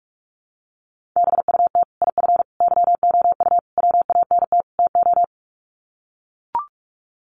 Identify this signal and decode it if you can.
Morse code, heard as a keyed tone, about 44 wpm, 700 Hz. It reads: BUT IF YOU WANT TO
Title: Sets of 5 Words Encore2 45wpm